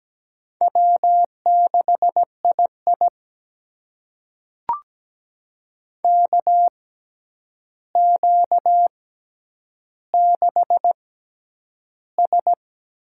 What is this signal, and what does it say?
Morse code, heard as a keyed tone, about 17 wpm, spaced as for 9 wpm, 700 Hz. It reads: W6II KQ6S
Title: USA Call Signs - 4 Characters - 6x Character Spacing - 17wpm